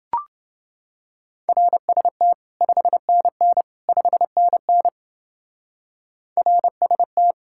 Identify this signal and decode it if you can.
Morse code, heard as a keyed tone, about 28 wpm, 700 Hz. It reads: RST 5NN 5NN RST